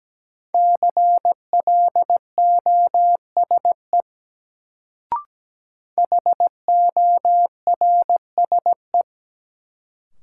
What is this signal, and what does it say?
Morse code, heard as a keyed tone, about 17 wpm, 700 Hz. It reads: CLOSE HORSE
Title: Sentences from Top 300 Words - New Word Drill - 17wpm